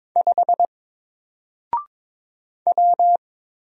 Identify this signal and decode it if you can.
Morse code, heard as a keyed tone, about 22 wpm, 700 Hz. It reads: 5 W
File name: Single-Letter-Number-22wpm